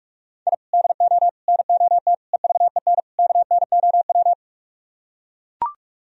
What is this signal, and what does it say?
Morse code, heard as a keyed tone, about 49 wpm, 700 Hz. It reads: I DO NOT EVEN KNOW